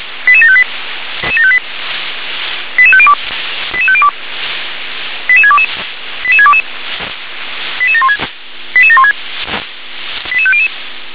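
Five-tone selective call (ZVEI), heard as quick runs of five tones, tones of about 70 ms. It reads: 88758 88502 88502 88622 88622 88716 88716 88500
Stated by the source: Five-tone ZVEI selective call received on an SDR.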